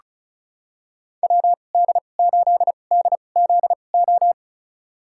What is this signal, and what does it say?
Morse code, heard as a keyed tone, about 36 wpm, spaced as for 25 wpm, 700 Hz. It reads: WD8DZO
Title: USA Call Signs - 2x Character Spacing - 35wpm